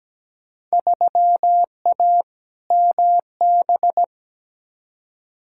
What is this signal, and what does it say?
Morse code, heard as a keyed tone, about 17 wpm, 700 Hz. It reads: 3A MB
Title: Field Day Exchanges Encore - 17wpm